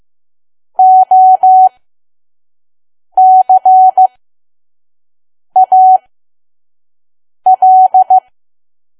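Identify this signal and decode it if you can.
Morse code, heard as a keyed tone, about 15 wpm, spaced as for 5 wpm, 750 Hz. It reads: OCAL
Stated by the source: Morse code practice at 5 WPM from KQ2P.COM.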